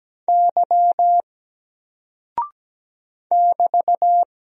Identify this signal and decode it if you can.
Morse code, heard as a keyed tone, about 17 wpm, 700 Hz. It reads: Y <BT>
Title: Single Character (Letter:Number:Punctuation:Prosign) 17wpm